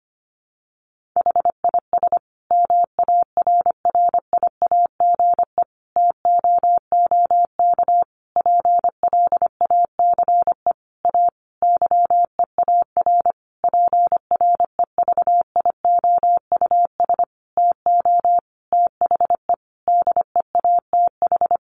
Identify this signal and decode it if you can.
Morse code, heard as a keyed tone, about 25 wpm, 700 Hz. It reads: HIS MARRIAGE TOOK PLACE A YEAR PREVIOUS TO THE DEATH
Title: Sentences from Top 1500 Words 25wpm